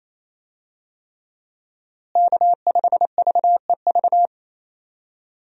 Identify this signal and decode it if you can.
Morse code, heard as a keyed tone, about 27 wpm, 700 Hz. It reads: K5VEV
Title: USA Call Signs - 5 Characters - Repeated 3x - 28wpm